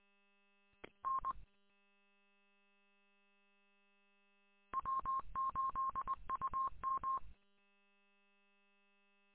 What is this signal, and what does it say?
Morse code, heard as a keyed tone, about 24 wpm, 1070 Hz. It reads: N W8UM
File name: w8um- -combined